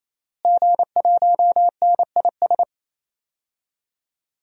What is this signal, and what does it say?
Morse code, heard as a keyed tone, about 27 wpm, 700 Hz. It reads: G1NIS
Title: DXCC-100 Call Signs - 5 Characters - Repeated 3x - 28wpm